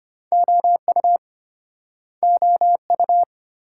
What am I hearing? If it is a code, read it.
Morse code, heard as a keyed tone, about 32 wpm, 700 Hz. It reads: OU OU